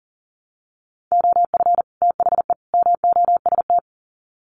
Morse code, heard as a keyed tone, about 40 wpm, 700 Hz. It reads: OF THE MOST
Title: Sets of 3 Words 40wpm